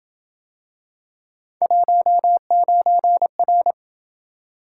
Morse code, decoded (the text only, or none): 19R